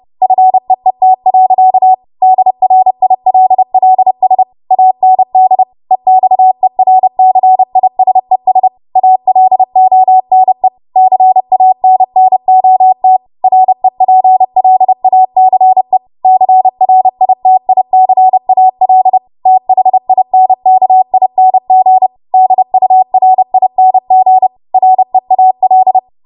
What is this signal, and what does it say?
Morse code, heard as a keyed tone, about 30 wpm, 750 Hz. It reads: FEET. DRILLS AND EXERCISES ALONE CANNOT REPLACE CRITICAL THINKING DURING REAL
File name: Morse / 30WPM